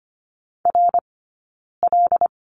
Morse code, heard as a keyed tone, about 25 wpm, spaced as for 9 wpm, 690 Hz. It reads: RL
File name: course / 25:10wpm / 092 Lesson 16d - Letter c - Two Characters 25wpm